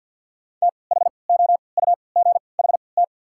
Morse code, heard as a keyed tone, about 53 wpm, spaced as for 29 wpm, 700 Hz. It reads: THOUGHT
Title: 3x Character Spacing - Top 300 Words - 50wpm